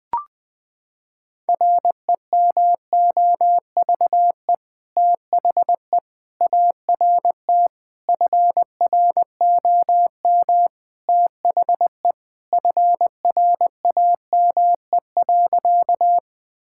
Morse code, heard as a keyed tone, about 20 wpm, 700 Hz. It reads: REMOVE THE ART FROM THE FRAME.
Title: Sentences from Top 1000 Words - Review Part 6 - 20wpm